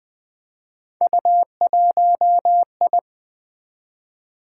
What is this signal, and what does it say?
Morse code, heard as a keyed tone, about 20 wpm, 700 Hz. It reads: U1I
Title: Three Letter-Number Combinations - 20wpm